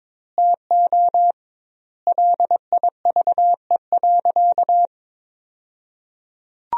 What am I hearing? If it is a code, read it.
Morse code, heard as a keyed tone, about 22 wpm, 700 Hz. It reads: TO LIVE.
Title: 2x Word-Spacing - Sentences from Top 300 Words - 22wpm